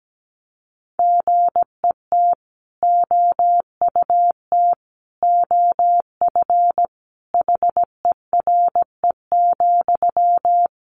Morse code, heard as a keyed tone, about 17 wpm, 700 Hz. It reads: GET OUT OF HERE,
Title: A Sound Of Thunder 17wpm